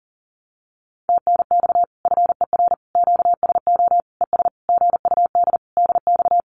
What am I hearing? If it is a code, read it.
Morse code, heard as a keyed tone, about 40 wpm, 700 Hz. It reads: TNX FER QSO ES GUD DX
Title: Common QSO Phrases Encore 40wpm